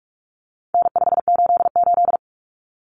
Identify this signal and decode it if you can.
Morse code, heard as a keyed tone, about 44 wpm, 700 Hz. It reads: N588
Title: Common Cut Numbers - 4-Digit 45wpm